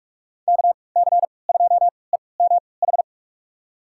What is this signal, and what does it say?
Morse code, heard as a keyed tone, about 46 wpm, spaced as for 25 wpm, 700 Hz. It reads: KC2EMH
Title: USA Call Signs - 6 Characters - 3x Character Spacing - 45wpm